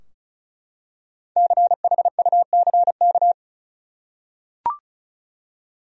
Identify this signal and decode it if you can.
Morse code, heard as a keyed tone, about 34 wpm, 700 Hz. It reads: CHUCK
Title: USA Common Names - Max 5 Letters - 35wpm